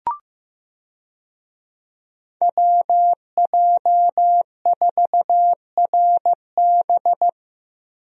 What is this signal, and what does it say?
Morse code, heard as a keyed tone, about 15 wpm, 700 Hz. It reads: WJ4RB
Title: USA Call Signs - 5 Characters - Repeated 3x - 15wpm